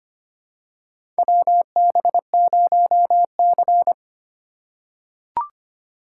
Morse code, heard as a keyed tone, about 25 wpm, 700 Hz. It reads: WB0C
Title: USA Call Signs - 4 Characters - 2x Character Spacing - 25wpm